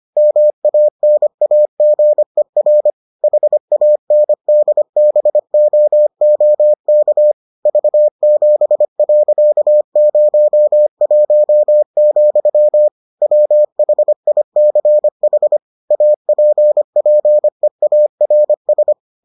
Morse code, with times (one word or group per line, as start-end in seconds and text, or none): 0.2-2.9 MANAGER
3.2-7.3 HANDBOOK
7.7-12.9 V7.01,
13.2-15.6 WHICH
15.9-18.9 APPEARS